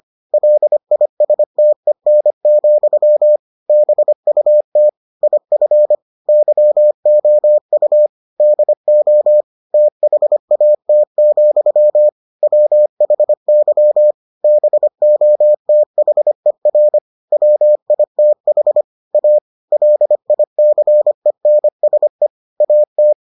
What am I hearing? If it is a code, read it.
Morse code, heard as a keyed tone, about 25 wpm, 600 Hz. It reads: LISTEN, BUT IF YOU DO THAT, WHY BOTHER WITH A LICENSE AT